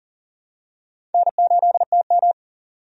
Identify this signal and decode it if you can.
Morse code, heard as a keyed tone, about 38 wpm, 700 Hz. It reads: N8TM
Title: USA Call Signs - 4 Characters - 40wpm